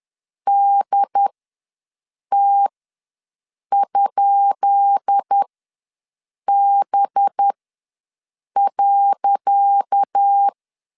Morse code, heard as a keyed tone, about 10 wpm, spaced as for 6 wpm, 800 Hz. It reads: DT?B.
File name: LEZIONI / 10 SIGNS / 10D COMA ws 10wpm 4314